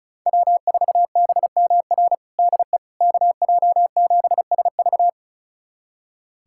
Morse code, heard as a keyed tone, about 34 wpm, 700 Hz. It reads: W4BMR DE KJ7SV
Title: USA Callsign DE Callsign - 5 Characters - 35wpm